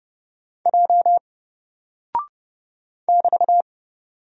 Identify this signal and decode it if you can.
Morse code, heard as a keyed tone, about 30 wpm, 700 Hz. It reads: J <BT>
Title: Single Character (Letter:Number:Punctuation:Prosign) Alt 30wpm